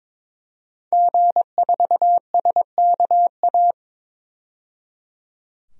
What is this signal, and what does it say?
Morse code, heard as a keyed tone, about 22 wpm, 700 Hz. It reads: G4SKA
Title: DXCC-100 Call Signs - 5 Characters - 22wpm